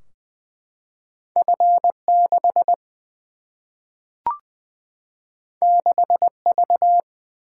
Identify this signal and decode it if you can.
Morse code, heard as a keyed tone, about 20 wpm, 700 Hz. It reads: F 6 6 V
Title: Single-Letter Single-Number 20wpm